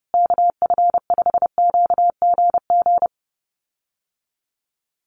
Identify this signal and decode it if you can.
Morse code, heard as a keyed tone, about 30 wpm, 700 Hz. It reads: KF5QGG
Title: USA Call Signs Repeated 2x 30wpm